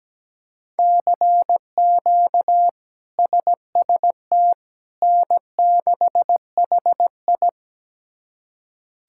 Morse code, heard as a keyed tone, about 17 wpm, 700 Hz. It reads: CQ SST N6HI
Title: K1USN SST QSO Practice - 17wpm